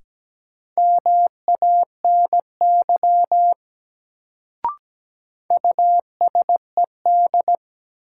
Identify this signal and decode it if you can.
Morse code, heard as a keyed tone, about 17 wpm, 700 Hz. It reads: MANY USED
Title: Sets of 2 Words - New Word Drill - 17wpm